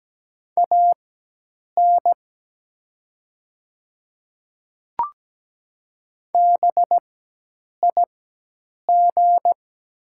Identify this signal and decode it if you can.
Morse code, heard as a keyed tone, about 17 wpm, spaced as for 8 wpm, 700 Hz. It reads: AN BIG
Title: 4x Character Spacing - Top 500 Words - 17wpm